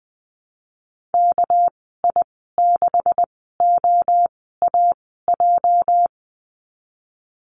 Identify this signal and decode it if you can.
Morse code, heard as a keyed tone, about 20 wpm, spaced as for 15 wpm, 700 Hz. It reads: KI6OAJ
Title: 207 Lesson 33g - Letter j - Call Signs 20wpm